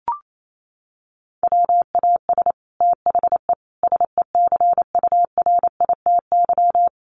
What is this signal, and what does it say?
Morse code, heard as a keyed tone, about 28 wpm, 700 Hz. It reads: WAS THE SECURITY
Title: Sentences from Top 1400 Words 28wpm